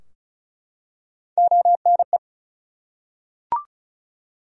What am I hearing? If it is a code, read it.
Morse code, heard as a keyed tone, about 36 wpm, 700 Hz. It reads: ONE
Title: Top 100 Words 35wpm